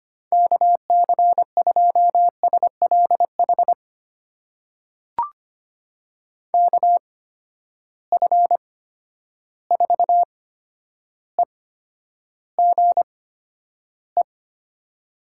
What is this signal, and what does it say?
Morse code, heard as a keyed tone, about 25 wpm, spaced as for 11 wpm, 700 Hz. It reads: KC2SLH KF4EGE
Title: USA Call Signs - 6 Characters - 8x Character Spacing - 25wpm